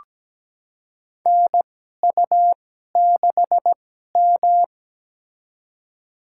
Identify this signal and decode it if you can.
Morse code, heard as a keyed tone, about 17 wpm, spaced as for 12 wpm, 700 Hz. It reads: NU6M
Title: USA Call Signs - 2x Character Spacing - 17wpm